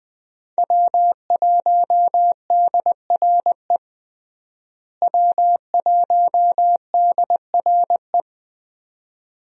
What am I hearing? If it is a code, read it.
Morse code, heard as a keyed tone, about 21 wpm, 700 Hz. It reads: W1DRE W1DRE
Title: USA Call Signs - 5 Characters - Repeated 2x - 20wpm